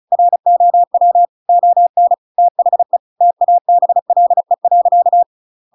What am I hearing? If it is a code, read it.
Morse code, heard as a keyed tone, about 34 wpm, 700 Hz. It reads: ROW ON THE TABLE.
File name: Pinocchio / Pinocchio ch7 35 35 ik1hge